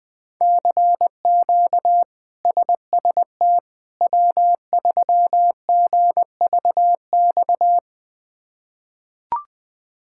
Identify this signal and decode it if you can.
Morse code, heard as a keyed tone, about 20 wpm, 700 Hz. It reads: CQ SST W3GVX
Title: K1USN SST QSO Practice - 20wpm